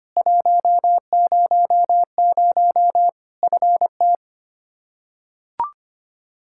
Morse code, heard as a keyed tone, about 25 wpm, 700 Hz. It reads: 100 FT
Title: Practice QSO - Exchange 2 - Info - 25wpm